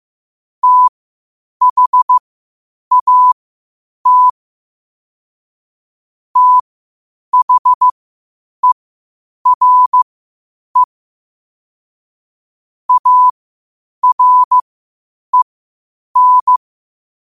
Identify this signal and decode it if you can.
Morse code, heard as a keyed tone, about 15 wpm, spaced as for 8 wpm, 1000 Hz. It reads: THAT THERE AREN
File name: qotdfiles / 8wpm